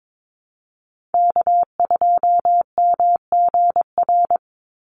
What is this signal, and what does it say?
Morse code, heard as a keyed tone, about 22 wpm, 700 Hz. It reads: K2MGR
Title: Call Signs - 22wpm